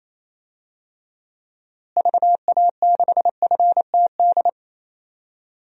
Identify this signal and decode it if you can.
Morse code, heard as a keyed tone, about 28 wpm, 700 Hz. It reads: VA6FTD
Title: DXCC-100 Call Signs - 6 Characters - Repeated 3x - 28wpm